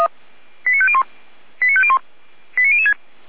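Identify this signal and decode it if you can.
Five-tone selective call (ZVEI), heard as quick runs of five tones, tones of about 70 ms. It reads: 89501 89501 89006
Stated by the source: ZVEI five-tone call received by SDR, from the fire brigade Feuerwehr Orlamünde.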